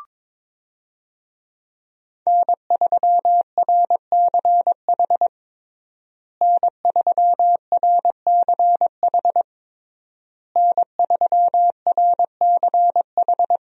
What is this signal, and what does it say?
Morse code, heard as a keyed tone, about 22 wpm, 700 Hz. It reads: N3RCH N3RCH N3RCH
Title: USA Call Signs - 5 Characters - Repeated 3x - 22wpm